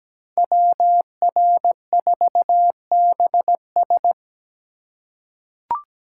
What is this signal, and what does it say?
Morse code, heard as a keyed tone, about 17 wpm, 700 Hz. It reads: WR4BS
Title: USA Call Signs - 5 Characters - 4x Character Spacing - 17wpm